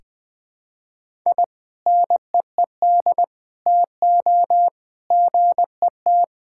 Morse code, heard as a keyed tone, about 21 wpm, 700 Hz. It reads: I NEED TO GET
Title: Sentences with 2-Word Phrases - Spoken Beginning - 20wpm